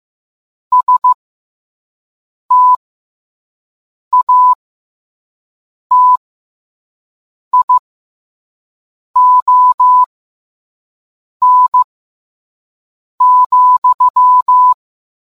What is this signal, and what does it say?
Morse code, heard as a keyed tone, about 15 wpm, spaced as for 5 wpm, 1000 Hz. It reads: STATION,